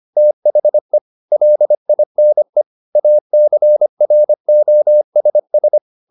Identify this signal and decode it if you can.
Morse code, heard as a keyed tone, about 25 wpm, 600 Hz. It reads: THE LINE ACROSS